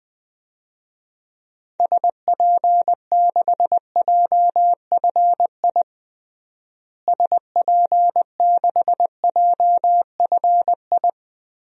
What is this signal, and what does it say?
Morse code, heard as a keyed tone, about 20 wpm, 700 Hz. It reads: SP6JFI SP6JFI
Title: DXCC-100 Call Signs - 6 Characters - Repeated 2x - 20wpm